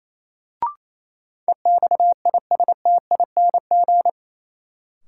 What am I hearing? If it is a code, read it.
Morse code, heard as a keyed tone, about 28 wpm, 700 Hz. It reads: EXISTING
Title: Sentences from Top 1500 Words - New Word Drill Alt - 28wpm